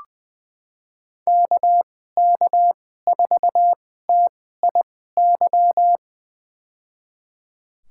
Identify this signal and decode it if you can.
Morse code, heard as a keyed tone, about 20 wpm, spaced as for 15 wpm, 700 Hz. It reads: KK4TIY
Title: USA Call Signs - 2x Character Spacing - 20wpm